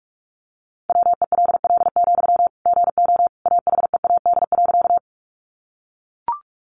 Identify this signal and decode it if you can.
Morse code, heard as a keyed tone, about 44 wpm, 700 Hz. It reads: WELL, GO AHEAD.
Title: Sentences from Top 1100 Words 45wpm